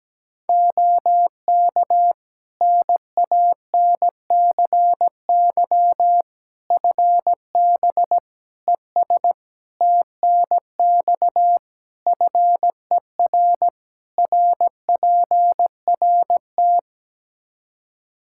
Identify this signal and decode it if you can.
Morse code, heard as a keyed tone, about 17 wpm, 700 Hz. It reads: OK NANCY FB ES TNX FER RPRT
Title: Practice QSO - Exchange 2 - Info - 17wpm